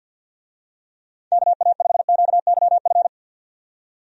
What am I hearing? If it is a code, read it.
Morse code, heard as a keyed tone, about 52 wpm, 700 Hz. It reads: KA5QYF
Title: USA Call Signs - 6 Characters - 50wpm